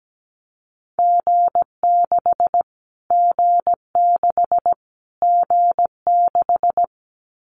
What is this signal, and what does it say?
Morse code, heard as a keyed tone, about 17 wpm, 700 Hz. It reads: G6 G6 G6
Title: Callsign Prefixes for Top 25 DXCC - Repeated 3x - 17wpm